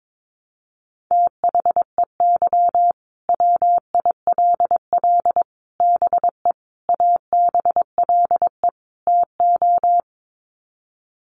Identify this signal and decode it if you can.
Morse code, heard as a keyed tone, about 22 wpm, 700 Hz. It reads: THEY WILL BE ABLE TO
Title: Sets of 5 Words Encore 22wpm